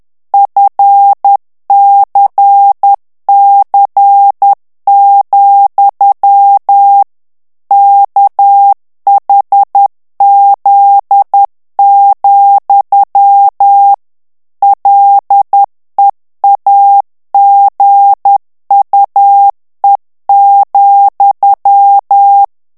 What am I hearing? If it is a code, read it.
Morse code, heard as a keyed tone, about 10 wpm, 800 Hz. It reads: FCC, KHZ, LEAGUE,
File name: ANGLAIS texte clair 3 vitesse 10 mots minute